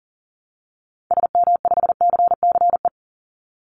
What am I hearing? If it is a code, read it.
Morse code, heard as a keyed tone, about 40 wpm, 700 Hz. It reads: SM5CCE
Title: Call Signs Repeated 3x - 40wpm